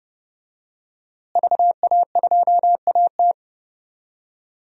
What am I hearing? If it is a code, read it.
Morse code, heard as a keyed tone, about 30 wpm, 700 Hz. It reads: VA2AT
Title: DXCC-100 Call Signs - 5 Characters - Repeated 2x - 30wpm